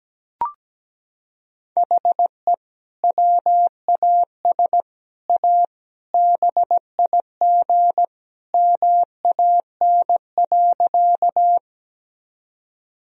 Morse code, heard as a keyed tone, about 17 wpm, 700 Hz. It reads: HE WAS A BIG MAN.
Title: Sentences from Top 1000 Words - Review Part 1 - 17wpm